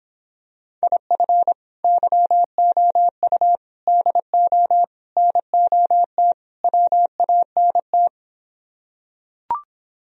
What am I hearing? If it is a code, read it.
Morse code, heard as a keyed tone, about 26 wpm, 700 Hz. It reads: IF YOU DO NOT WANT